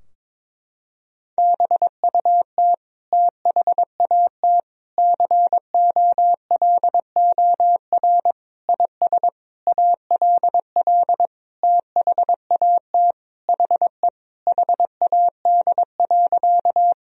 Morse code, heard as a keyed tone, about 22 wpm, 700 Hz. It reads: BUT THAT COLOR IS ALL THAT HE HAD.
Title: Sentences from Top 1000 Words - Review Part 2 - 22wpm